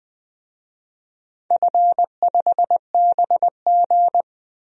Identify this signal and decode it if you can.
Morse code, heard as a keyed tone, about 20 wpm, 700 Hz. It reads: F5BG
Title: DXCC-100 Call Signs - 4 Characters - 20wpm